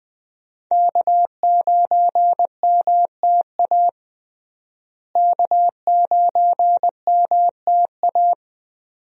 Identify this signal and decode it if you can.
Morse code, heard as a keyed tone, about 20 wpm, 700 Hz. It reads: K9MTA K9MTA
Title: USA Call Signs - 5 Characters - Repeated 2x - 20wpm